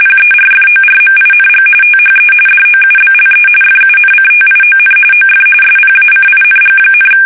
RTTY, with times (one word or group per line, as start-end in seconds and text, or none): 0.0-1.6 PERFORMING
1.8-2.3 THE
2.5-3.5 DOUBLE
3.6-5.0 FUNCTION
5.1-5.4 OF
5.6-7.1 SPACING,